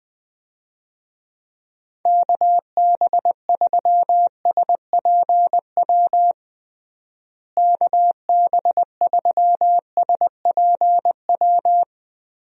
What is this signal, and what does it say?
Morse code, heard as a keyed tone, about 19 wpm, 700 Hz. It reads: KB3SPW KB3SPW
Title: USA Call Signs - 6 Characters - Repeated 3x - 20wpm